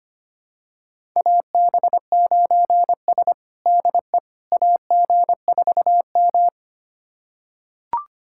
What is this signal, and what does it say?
Morse code, heard as a keyed tone, about 25 wpm, 700 Hz. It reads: AB9S DE AG4M
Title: USA Callsign DE Callsign - 4 Characters - 25wpm